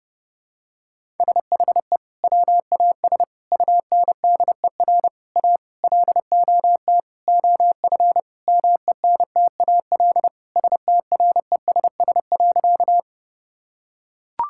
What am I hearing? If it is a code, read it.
Morse code, heard as a keyed tone, about 32 wpm, 700 Hz. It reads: SHE WAS UNDER A LOT OF MENTAL STRESS.
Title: Sentences from Top 1000 Words - Review Part 6 - 30wpm